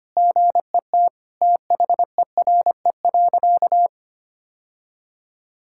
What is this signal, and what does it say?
Morse code, heard as a keyed tone, about 25 wpm, 700 Hz. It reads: GET THERE.
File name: practice / 25wpm / Sentences with 2-Word Phrases - 25wpm